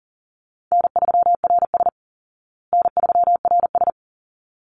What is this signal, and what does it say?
Morse code, heard as a keyed tone, about 40 wpm, 700 Hz. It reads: N3RS N3RS
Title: Call Signs Repeated 3x - 40wpm